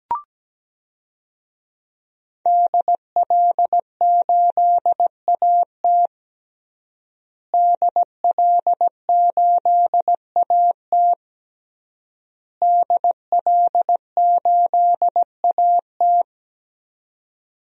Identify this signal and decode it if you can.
Morse code, heard as a keyed tone, about 17 wpm, 700 Hz. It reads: DL8AT DL8AT DL8AT
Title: DXCC-100 Call Signs - 5 Characters - Repeated 3x - 17wpm